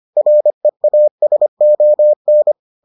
Morse code, heard as a keyed tone, about 25 wpm, 600 Hz. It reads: REASON